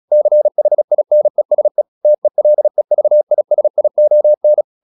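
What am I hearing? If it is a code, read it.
Morse code, heard as a keyed tone, about 36 wpm, 600 Hz. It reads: CHINESE TELEVISION
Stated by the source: Morse practice at 36 wpm, text of Ken Follett's novel Never, from CW med Gnister.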